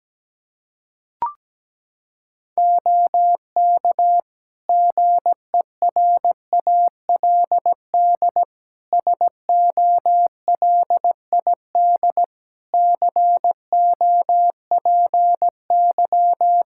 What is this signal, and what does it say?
Morse code, heard as a keyed tone, about 17 wpm, 700 Hz. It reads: OK GERALD SOLID COPY
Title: Practice QSO - Exchange 3 - More Info - 17wpm